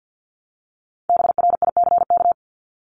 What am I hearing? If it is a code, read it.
Morse code, heard as a keyed tone, about 50 wpm, 700 Hz. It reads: BRICK
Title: Top 1751-2000 Words 50wpm